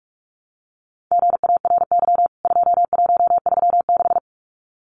Morse code, heard as a keyed tone, about 44 wpm, 700 Hz. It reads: GARY 2136
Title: CWT Exchange 45wpm